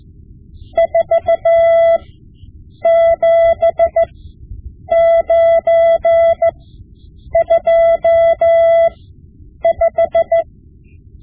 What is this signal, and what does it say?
Morse code, heard as a keyed tone, about 15 wpm, spaced as for 8 wpm, 670 Hz. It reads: 47925
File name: Morse Lessons / Morse Lessons / Tape-5B